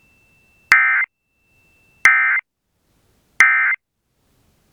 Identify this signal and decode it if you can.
Emergency Alert System (SAME) tones, heard as harsh buzzing bursts. It is heard from 0.7 to 3.7 s.